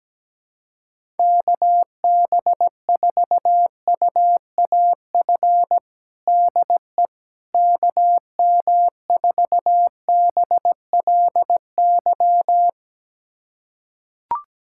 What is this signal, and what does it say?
Morse code, heard as a keyed tone, about 17 wpm, 700 Hz. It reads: KB4UAF DE KM4BLY
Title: USA Callsign DE Callsign 17wpm